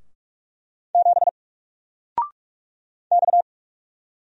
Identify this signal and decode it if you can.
Morse code, heard as a keyed tone, about 44 wpm, 700 Hz. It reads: 7 X
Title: Single Letter-Number Alt 45wpm